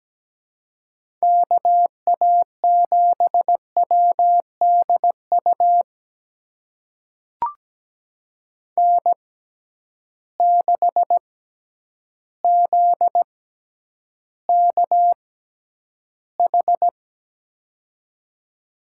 Morse code, heard as a keyed tone, about 17 wpm, spaced as for 9 wpm, 700 Hz. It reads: KA7WDU N6ZKH
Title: USA Call Signs - 6x Character Spacing - 17wpm